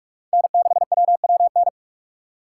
Morse code, heard as a keyed tone, about 43 wpm, 700 Hz. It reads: N6WWN
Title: USA Call Signs - 5 Characters - 6x Character Spacing - 45wpm